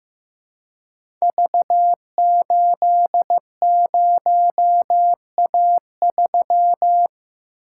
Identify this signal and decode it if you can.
Morse code, heard as a keyed tone, about 15 wpm, 700 Hz. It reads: V80A3
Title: Code Groups - 5 Letter:Numbers - 15wpm